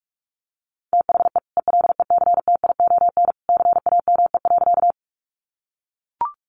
Morse code, heard as a keyed tone, about 44 wpm, 700 Hz. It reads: THE ELECTION CAME.
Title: Sentences from Top 1500 Words 45wpm